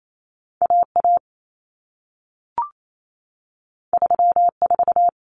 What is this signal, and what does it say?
Morse code, heard as a keyed tone, about 28 wpm, 700 Hz. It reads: AA 34
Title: Common Cut Numbers - 2-Digit 28wpm